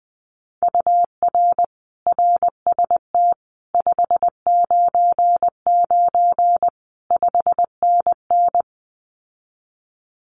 Morse code, heard as a keyed tone, about 20 wpm, 700 Hz. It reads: UR RST 599 5NN
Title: Common QSO Phrases Encore 20wpm